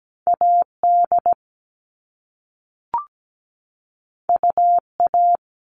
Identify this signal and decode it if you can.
Morse code, heard as a keyed tone, about 17 wpm, 700 Hz. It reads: AD UA
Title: All Cut Numbers - 2-Digit 17wpm